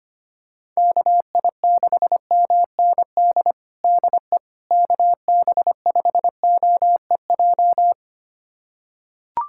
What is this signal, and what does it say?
Morse code, heard as a keyed tone, about 25 wpm, 700 Hz. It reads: KI6MND DE KB5OEJ
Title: USA Callsign DE Callsign 25wpm